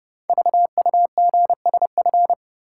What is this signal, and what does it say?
Morse code, heard as a keyed tone, about 28 wpm, 700 Hz. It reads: VUGSF
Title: Code Groups - 5 Letters - 30wpm